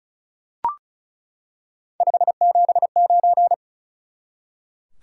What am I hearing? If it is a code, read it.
Morse code, heard as a keyed tone, about 36 wpm, 700 Hz. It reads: 579
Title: Top 180 QSO Elements 35wpm